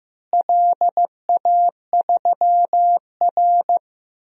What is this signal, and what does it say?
Morse code, heard as a keyed tone, about 15 wpm, 700 Hz. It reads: LA3R
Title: DXCC-100 Call Signs - 4 Characters - Repeated 3x - 15wpm